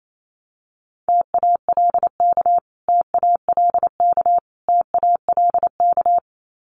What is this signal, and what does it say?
Morse code, heard as a keyed tone, about 28 wpm, 700 Hz. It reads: TALK TALK TALK
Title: Top 500 Words Repeated 28wpm